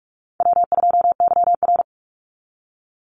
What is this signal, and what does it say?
Morse code, heard as a keyed tone, about 44 wpm, 700 Hz. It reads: W2YR
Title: Call Signs Repeated 3x - 45wpm